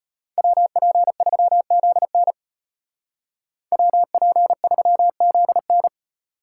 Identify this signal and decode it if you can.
Morse code, heard as a keyed tone, about 30 wpm, 700 Hz. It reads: WP3ZN WP3ZN
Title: Speed Racing - USA Call Signs 25wpm